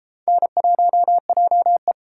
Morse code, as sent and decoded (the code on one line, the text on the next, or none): -. .---- .--- .
N1JE